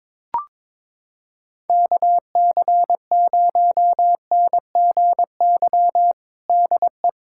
Morse code, heard as a keyed tone, about 22 wpm, 700 Hz. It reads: KC0NGY DE
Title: USA Callsign DE Callsign - 6 Characters - 22wpm